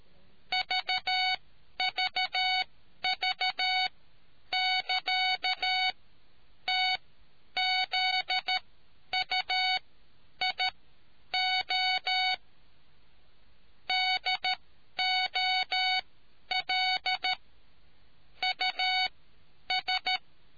Morse code, heard as a keyed tone, about 13 wpm, spaced as for 9 wpm, 2250 Hz. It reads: VVV<KA>TZUIO DOLUS